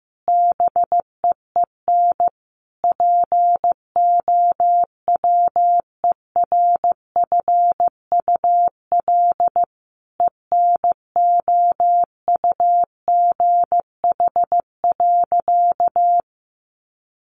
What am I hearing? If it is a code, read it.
Morse code, heard as a keyed tone, about 15 wpm, 700 Hz. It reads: BEEN POWERFUL ENOUGH.
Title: Sentences from Top 1300 Words 15wpm